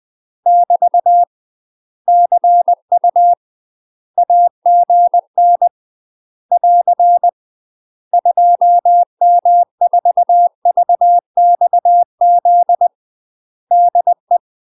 Morse code, heard as a keyed tone, about 20 wpm, 700 Hz. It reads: <BT> CU AGN <AR> 2M4VXZ DE
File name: UK MASTER QSOs TRAINING/20WPM